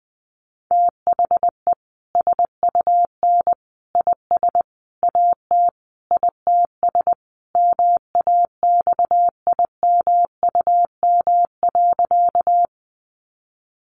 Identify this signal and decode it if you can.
Morse code, heard as a keyed tone, about 20 wpm, 700 Hz. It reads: THE SUN IS AT ITS MAXIMUM.
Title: Sentences from Top 1300 Words 20wpm